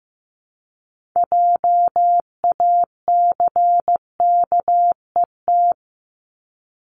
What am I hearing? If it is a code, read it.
Morse code, heard as a keyed tone, about 15 wpm, 700 Hz. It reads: JACKET
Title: Multi-letter Phonemes Repeated 3x 15wpm